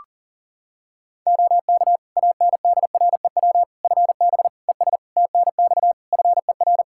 Morse code, heard as a keyed tone, about 37 wpm, 700 Hz. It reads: OK ANDREW FB ES TNX FER
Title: Practice QSO - Exchange 2 - Info - 40wpm